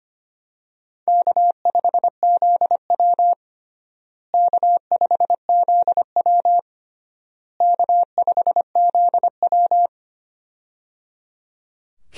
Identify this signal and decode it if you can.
Morse code, heard as a keyed tone, about 25 wpm, 700 Hz. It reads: K5ZW K5ZW K5ZW
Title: USA Call Signs - 4 Characters - Repeated 3x - 25wpm